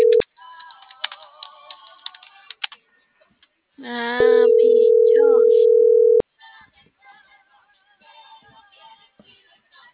Ringback tone (North America), 440 + 480 Hz.